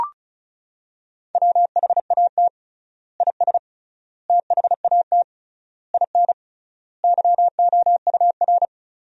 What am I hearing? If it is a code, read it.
Morse code, heard as a keyed tone, about 34 wpm, 700 Hz. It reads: WHAT IS THAT IN YOUR
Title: 3x Word-Spacing - Sentences from Top 200 Words - 35wpm